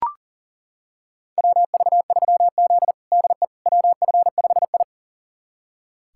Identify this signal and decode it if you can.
Morse code, heard as a keyed tone, about 40 wpm, 700 Hz. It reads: WV3Z DE WF5I
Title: USA Callsign DE Callsign - 4 Characters - 40wpm